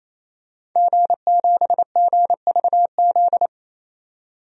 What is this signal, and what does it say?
Morse code, heard as a keyed tone, about 28 wpm, 700 Hz. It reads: G7GVZ
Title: DXCC-100 Call Signs - 5 Characters - 28wpm